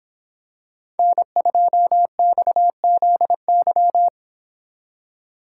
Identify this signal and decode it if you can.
Morse code, heard as a keyed tone, about 25 wpm, 700 Hz. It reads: N2XZY